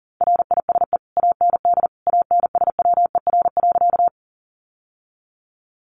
Morse code, heard as a keyed tone, about 37 wpm, 700 Hz. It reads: RISE AND ANSWER.
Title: Sentences from Top 900 Words 40wpm